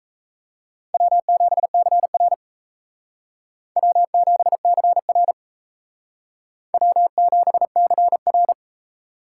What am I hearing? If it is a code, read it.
Morse code, heard as a keyed tone, about 35 wpm, 700 Hz. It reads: W7CR W7CR W7CR
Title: Speed Racing - USA 1x2 Call Signs 28wpm